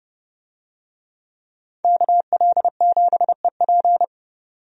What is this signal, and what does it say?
Morse code, heard as a keyed tone, about 30 wpm, 700 Hz. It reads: KL7EP
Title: DXCC-100 Call Signs - 5 Characters - 30wpm